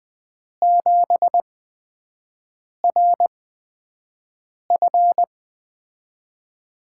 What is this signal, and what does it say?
Morse code, heard as a keyed tone, about 20 wpm, spaced as for 5 wpm, 700 Hz. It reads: 7RF
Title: USA Call Signs - 4 Characters - 8x Character Spacing - 20wpm